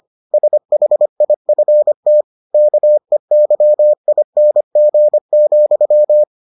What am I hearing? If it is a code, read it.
Morse code, heard as a keyed tone, about 25 wpm, 600 Hz. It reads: SHIFT KEYING,